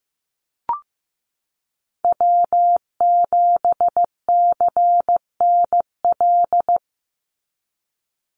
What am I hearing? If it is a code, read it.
Morse code, heard as a keyed tone, about 15 wpm, 700 Hz. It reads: W7CNL
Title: NA SOTA Chaser Callsigns 15wpm